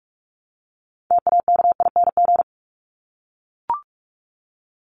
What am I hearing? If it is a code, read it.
Morse code, heard as a keyed tone, about 44 wpm, 700 Hz. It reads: TAKING
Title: ING Suffix - Top 100 Words with Base Words 45wpm